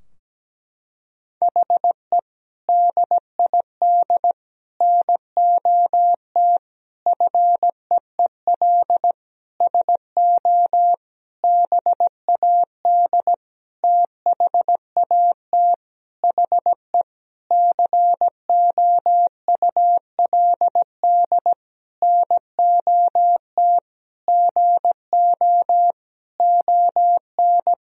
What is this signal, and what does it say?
Morse code, heard as a keyed tone, about 17 wpm, 700 Hz. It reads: HE DID NOT FEEL SO BAD THAT HE COULD NOT GO ON
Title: Sentences from Top 1000 Words - Review Part 4 - 17wpm